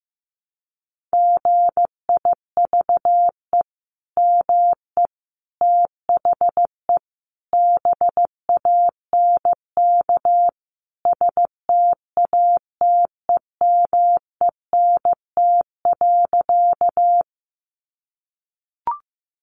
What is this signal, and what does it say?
Morse code, heard as a keyed tone, about 15 wpm, 700 Hz. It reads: GIVE ME THE BANK STATEMENT.
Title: Sentences from Top 1200 Words 15wpm